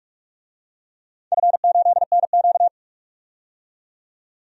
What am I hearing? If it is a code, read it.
Morse code, heard as a keyed tone, about 42 wpm, 700 Hz. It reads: F8NQ
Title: DXCC-100 Call Signs - 4 Characters - 45wpm